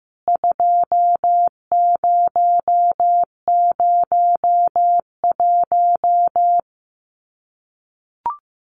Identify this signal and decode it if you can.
Morse code, heard as a keyed tone, about 15 wpm, 700 Hz. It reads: 2001
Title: CWT Exchange 15wpm